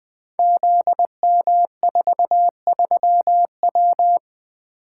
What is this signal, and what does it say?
Morse code, heard as a keyed tone, about 20 wpm, 700 Hz. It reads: ZM43W
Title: Code Groups - 5 Letter:Numbers - 20wpm